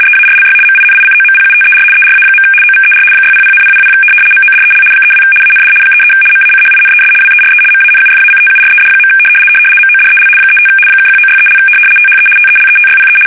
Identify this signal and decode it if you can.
RTTY, about 75 baud. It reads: EARLY MINICOMPUTER ERA. OVER 600,000 MODEL 32 AND 33 MACHINES WERE MANUFACTURED. MODEL 35 (1963) THE TELETYPE
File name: fsk850-75B / teletype-corp